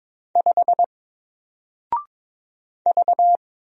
Morse code, heard as a keyed tone, about 22 wpm, 700 Hz. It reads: 5 V
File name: Single-Letter-Number-22wpm